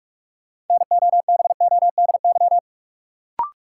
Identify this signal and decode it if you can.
Morse code, heard as a keyed tone, about 46 wpm, 700 Hz. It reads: NOBODY
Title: Sentences from Top 1600 Words - New Word Drill Alt - 45wpm